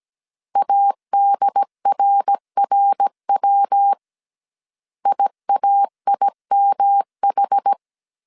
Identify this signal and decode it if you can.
Morse code, heard as a keyed tone, about 17 wpm, 800 Hz. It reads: ADRRW IAIMH